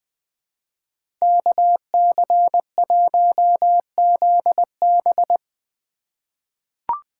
Morse code, heard as a keyed tone, about 20 wpm, 700 Hz. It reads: KC1ZB
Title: USA Call Signs - 5 Characters - 2x Character Spacing - 20wpm